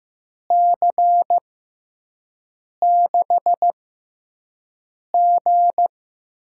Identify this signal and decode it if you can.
Morse code, heard as a keyed tone, about 15 wpm, spaced as for 5 wpm, 700 Hz. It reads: C6G
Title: USA Call Signs - 6 Characters - 6x Character Spacing - 15wpm